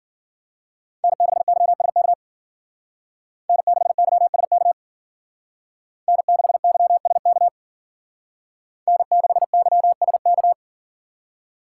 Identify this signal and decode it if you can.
Morse code, heard as a keyed tone, about 50 wpm, 700 Hz. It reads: N6YSK N6YSK N6YSK N6YSK
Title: Speed Racing - USA 1x3 Call Signs 40wpm